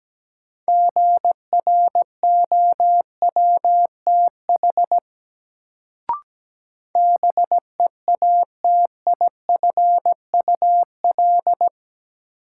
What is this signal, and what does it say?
Morse code, heard as a keyed tone, about 17 wpm, 700 Hz. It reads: GROWTH BEATIFUL
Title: Sentences from Top 1300 Words - New Word Drill Alt - 17wpm